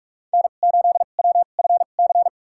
Morse code, heard as a keyed tone, about 46 wpm, spaced as for 33 wpm, 700 Hz. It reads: N8WFC
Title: USA Call Signs - 5 Characters - 2x Character Spacing - 45wpm